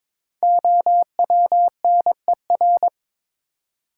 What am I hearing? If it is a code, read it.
Morse code, heard as a keyed tone, about 22 wpm, 700 Hz. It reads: OWNER
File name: 5 Letter Words - First 3 Letters Given - 22wpm